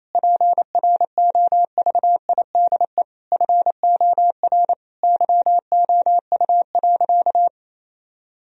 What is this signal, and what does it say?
Morse code, heard as a keyed tone, about 28 wpm, 700 Hz. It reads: PROVIDE FOR YOU.
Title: Sentences from Top 1000 Words - Review Part 3 - 28wpm